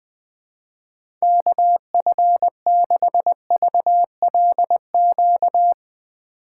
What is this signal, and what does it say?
Morse code, heard as a keyed tone, about 20 wpm, 700 Hz. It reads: KF6VLQ
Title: USA Call Signs - 6 Characters - 3x Character Spacing - 20wpm